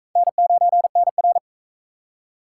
Morse code, heard as a keyed tone, about 39 wpm, 700 Hz. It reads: N9NR